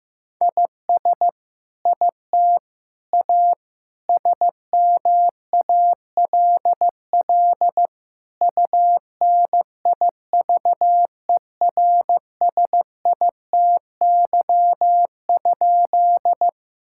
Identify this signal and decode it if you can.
Morse code, heard as a keyed tone, about 15 wpm, 700 Hz. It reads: IS IT A SMALL UNIVERSITY?
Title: Sentences from Top 1000 Words - Review Part 5 - 15wpm